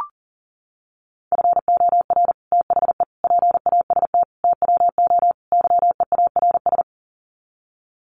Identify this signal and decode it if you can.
Morse code, heard as a keyed tone, about 38 wpm, 700 Hz. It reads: FOR THE PAST TWO YEARS
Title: Sets of 5 Words Encore2 40wpm